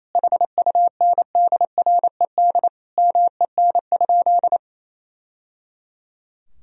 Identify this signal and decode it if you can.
Morse code, heard as a keyed tone, about 28 wpm, 700 Hz. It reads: HUNDRED MEN?
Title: Sentences from Top 1000 Words - Review Part 2 - 28wpm